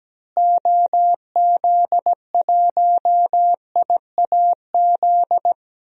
Morse code, heard as a keyed tone, about 17 wpm, 700 Hz. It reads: OZ1IAZ
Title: DXCC-100 Call Signs - 6 Characters - Repeated 2x - 17wpm